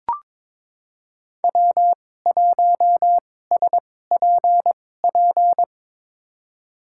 Morse code, heard as a keyed tone, about 22 wpm, spaced as for 16 wpm, 700 Hz. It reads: W1SPP
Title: USA Call Signs - 5 Characters - 2x Character Spacing - 22wpm